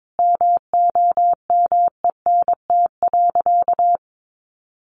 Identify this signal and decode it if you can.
Morse code, heard as a keyed tone, about 22 wpm, 700 Hz. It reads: MOMENT.
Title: Sentences from Top 1000 Words 22wpm